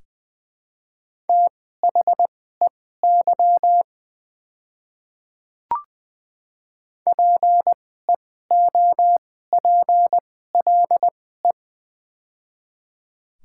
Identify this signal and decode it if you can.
Morse code, heard as a keyed tone, about 20 wpm, spaced as for 15 wpm, 700 Hz. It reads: THEY PEOPLE
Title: 2x Character Spacing - Top 500 Words - 20wpm